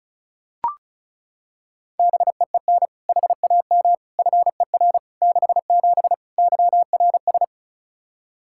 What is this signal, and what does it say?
Morse code, heard as a keyed tone, about 34 wpm, 700 Hz. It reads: BEEN HAM FER 67 YRS
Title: Practice QSO - Exchange 3 - More Info - 35wpm